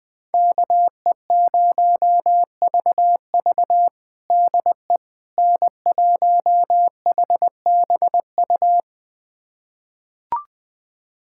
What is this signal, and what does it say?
Morse code, heard as a keyed tone, about 20 wpm, 700 Hz. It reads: KE0VV DE N1HBU
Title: USA Callsign DE Callsign - 5 Characters - 20wpm